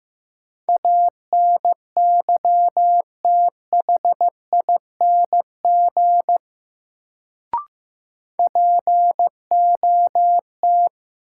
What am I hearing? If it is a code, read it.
Morse code, heard as a keyed tone, about 15 wpm, 700 Hz. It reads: ANYTHING POT
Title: Sentences from Top 1000 Words - New Word Drill - 15wpm